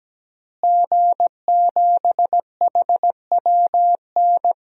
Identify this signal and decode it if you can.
Morse code, heard as a keyed tone, about 17 wpm, 700 Hz. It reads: G7HWN
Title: DXCC-100 Call Signs - 5 Characters - Repeated 2x - 17wpm